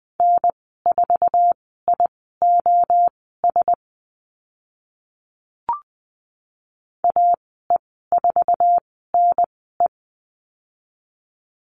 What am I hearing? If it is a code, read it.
Morse code, heard as a keyed tone, about 20 wpm, spaced as for 14 wpm, 700 Hz. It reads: N4IOS AE4NE
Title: 062 Lesson 11g - Letter h - Call Signs 20wpm